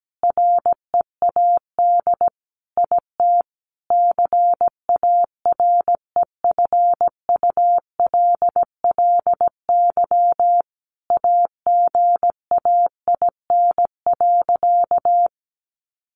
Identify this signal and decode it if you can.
Morse code, heard as a keyed tone, about 17 wpm, 700 Hz. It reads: READ IT CAREFULLY AGAIN.
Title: Sentences from Top 1300 Words 17wpm